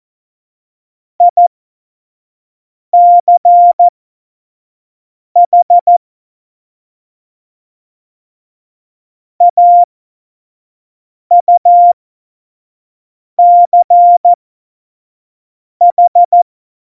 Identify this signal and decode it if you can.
Morse code, heard as a keyed tone, about 14 wpm, spaced as for 5 wpm, 700 Hz. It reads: ICH AUCH
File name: Morsekurs / Witze / Witz 11 August 2020, 25 F70